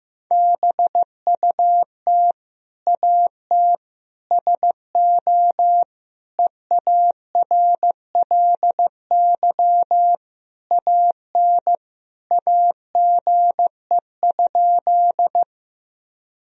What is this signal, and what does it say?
Morse code, heard as a keyed tone, about 15 wpm, 700 Hz. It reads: BUT AT SO EARLY AN AGE?
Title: Sentences from Top 1000 Words - Review Part 3 - 15wpm